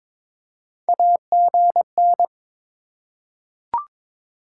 Morse code, heard as a keyed tone, about 22 wpm, 700 Hz. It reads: AGN
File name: practice / 22wpm / Top 20 QSO Elements - 2x Character Spacing - 22wpm